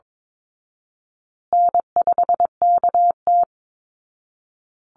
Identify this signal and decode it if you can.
Morse code, heard as a keyed tone, about 22 wpm, 700 Hz. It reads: N5KT
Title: Easy Call Signs Repeated 3x - 22wpm